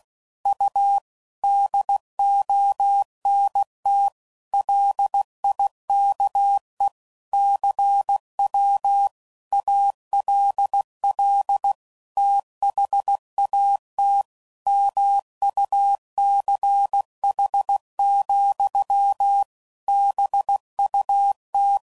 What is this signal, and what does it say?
Morse code, heard as a keyed tone, about 16 wpm, 800 Hz. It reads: U DONT LIKE CW ALL THAT MUCH, BUT
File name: CW Code Groups @15WPM / QSOs / WD0FBT